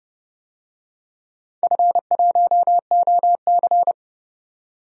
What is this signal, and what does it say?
Morse code, heard as a keyed tone, about 30 wpm, 700 Hz. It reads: F1OC
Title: DXCC-100 Call Signs - 4 Characters - 30wpm